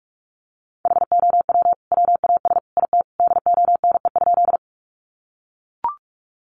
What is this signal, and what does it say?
Morse code, heard as a keyed tone, about 44 wpm, 700 Hz. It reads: HOW WAS IT DONE?
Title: Sentences from Top 500 Words 45wpm